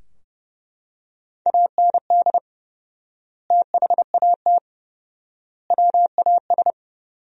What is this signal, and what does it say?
Morse code, heard as a keyed tone, about 30 wpm, 700 Hz. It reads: AND THAT WAS
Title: 4x Word-Spacing - Sentences from Top 200 Words - 30wpm